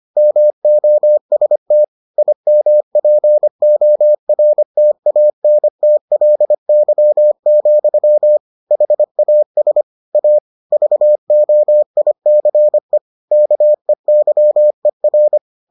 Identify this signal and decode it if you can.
Morse code, heard as a keyed tone, about 25 wpm, 600 Hz. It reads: MOST IMPORTANTLY, HAS A VOICE KEYER